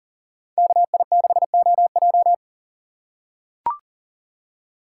Morse code, heard as a keyed tone, about 40 wpm, 700 Hz. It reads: KI6OJ
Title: USA Call Signs - 5 Characters - 4x Character Spacing - 40wpm